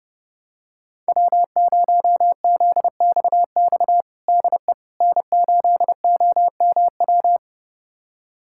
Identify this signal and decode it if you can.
Morse code, heard as a keyed tone, about 30 wpm, 700 Hz. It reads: W0ZXX DE N8OMW
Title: USA Callsign DE Callsign - 5 Characters - 30wpm